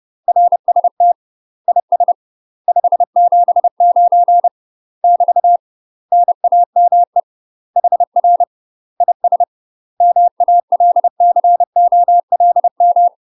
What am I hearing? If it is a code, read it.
Morse code, heard as a keyed tone, about 30 wpm, 700 Hz. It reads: RST IS 579 <BT> NAME HR IS MALCOLM
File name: UK MASTER QSOs TRAINING / 30WPM / UK MASTER QSOs 0008